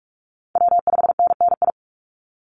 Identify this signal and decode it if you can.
Morse code, heard as a keyed tone, about 44 wpm, 700 Hz. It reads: W5NNI
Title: USA Call Signs 45wpm